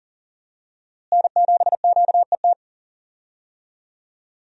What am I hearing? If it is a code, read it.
Morse code, heard as a keyed tone, about 40 wpm, 700 Hz. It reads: N7QET
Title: USA Call Signs - 5 Characters - Repeated 2x - 40wpm